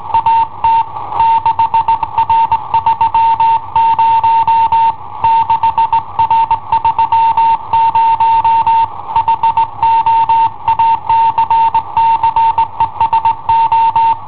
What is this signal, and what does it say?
Morse code, heard as a keyed tone, about 19 wpm, 910 Hz. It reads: AT 6R30 6R30 HO ACCESO
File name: i7ohp / cassetta i7ohp B